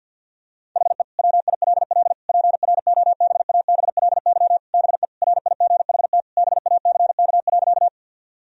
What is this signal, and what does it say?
Morse code, heard as a keyed tone, about 50 wpm, 700 Hz. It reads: HE WILL PROBABLY BE RIGHT BACK.
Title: Sentences from Top 1000 Words - Review Part 4 - 50wpm